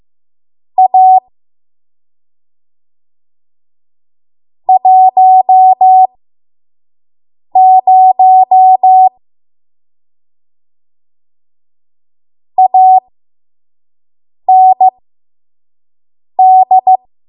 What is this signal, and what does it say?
Morse code, heard as a keyed tone, about 15 wpm, spaced as for 5 wpm, 750 Hz. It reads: A 10 AND
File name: Morse / 05WPM